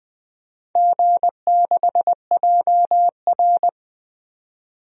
Morse code, heard as a keyed tone, about 21 wpm, 700 Hz. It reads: G6JR
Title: DXCC-100 Call Signs - 4 Characters - 20wpm